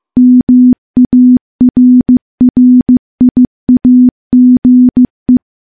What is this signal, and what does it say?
Morse code, heard as a keyed tone, about 15 wpm, 250 Hz. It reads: MARRIAGE
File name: morse-training-15WPM